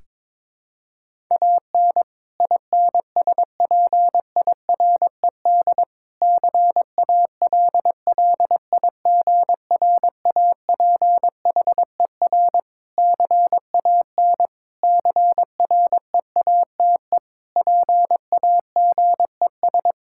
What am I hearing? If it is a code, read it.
Morse code, heard as a keyed tone, about 22 wpm, 700 Hz. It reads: AN INSPIRED CALLIGRAPHER CAN CREATE PAGES
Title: Pangram Challenge v2 - 22wpm